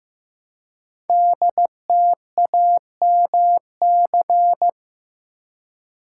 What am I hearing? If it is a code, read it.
Morse code, heard as a keyed tone, about 15 wpm, 700 Hz. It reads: DTAMC
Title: Code Groups - 5 Letters - 15wpm